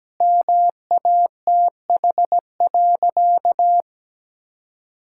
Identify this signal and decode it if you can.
Morse code, heard as a keyed tone, about 17 wpm, 700 Hz. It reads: MATH.
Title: Sentences from Top 1000 Words - Review Part 6 - 17wpm